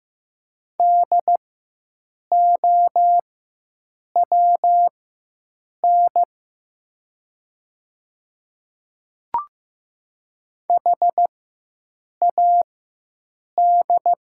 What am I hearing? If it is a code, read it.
Morse code, heard as a keyed tone, about 15 wpm, spaced as for 7 wpm, 700 Hz. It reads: DOWN HAD
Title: 4x Character Spacing - Top 100 Words - 15wpm